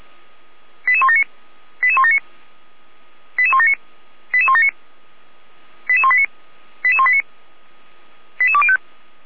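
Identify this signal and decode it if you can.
Five-tone selective call (ZVEI), heard as quick runs of five tones, tones of about 70 ms. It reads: 88179 88179 88179 88179 88189 88189 88205
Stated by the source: ZVEI five-tone call received on an SDR.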